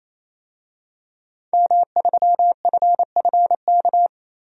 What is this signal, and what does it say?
Morse code, heard as a keyed tone, about 28 wpm, 700 Hz. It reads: M3FFK
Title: DXCC-100 Call Signs - 5 Characters - 28wpm